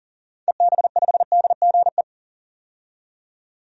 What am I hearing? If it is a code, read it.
Morse code, heard as a keyed tone, about 40 wpm, 700 Hz. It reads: EB5DGE
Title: DXCC-100 Call Signs - 6 Characters - 40wpm